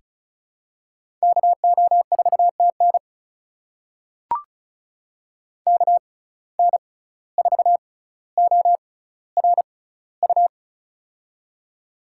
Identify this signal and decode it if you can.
Morse code, heard as a keyed tone, about 36 wpm, spaced as for 17 wpm, 700 Hz. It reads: KO4TN KN4ORU
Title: USA Call Signs - 6x Character Spacing - 35wpm